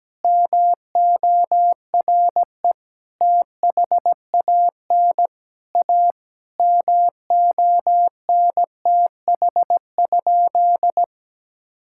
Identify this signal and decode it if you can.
Morse code, heard as a keyed tone, about 17 wpm, 700 Hz. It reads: MORE THAN A MONTH?
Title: Sentences from Top 1000 Words - Review Part 5 - 17wpm